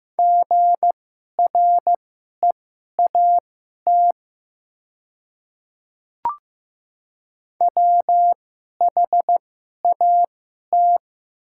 Morse code, heard as a keyed tone, about 15 wpm, spaced as for 11 wpm, 700 Hz. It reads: GREAT WHAT
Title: 2x Character Spacing - Top 200 Words - 15wpm